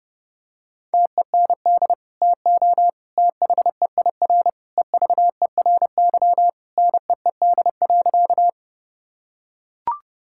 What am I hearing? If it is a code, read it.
Morse code, heard as a keyed tone, about 29 wpm, 700 Hz. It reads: TEND TO THEIR EVERY NEED.
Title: Sentences from Top 1000 Words - Review Part 5 - 30wpm